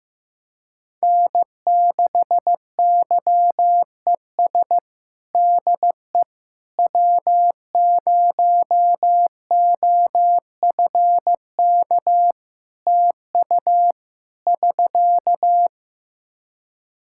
Morse code, heard as a keyed tone, about 16 wpm, 700 Hz. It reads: N6YES DE W0OFK TU <SK>
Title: K1USN SST QSO Practice 15wpm